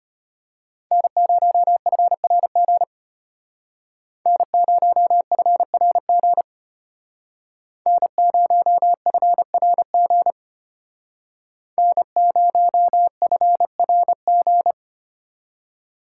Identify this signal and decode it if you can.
Morse code, heard as a keyed tone, about 31 wpm, 700 Hz. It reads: N0FRG N0FRG N0FRG N0FRG
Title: Speed Racing - USA 1x3 Call Signs 25wpm